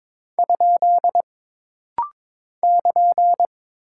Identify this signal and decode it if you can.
Morse code, heard as a keyed tone, about 22 wpm, 700 Hz. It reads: ? <KN>
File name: practice / 22wpm / Single Punctuation-Prosign - Rapid-Fire - 22wpm